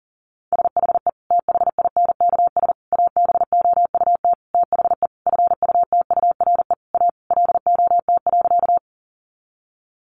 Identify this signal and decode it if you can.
Morse code, heard as a keyed tone, about 37 wpm, 700 Hz. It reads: SHE THINKS ABOUT THE FUTURE A LOT.
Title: Easy Phrases 40wpm